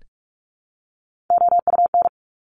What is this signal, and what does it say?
Morse code, heard as a keyed tone, about 46 wpm, 700 Hz. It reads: OUN
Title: Top Three Letter Combinations in Words 45wpm